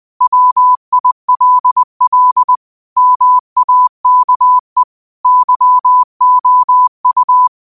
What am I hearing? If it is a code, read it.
Morse code, heard as a keyed tone, about 20 wpm, 1000 Hz. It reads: WILL MAKE YOU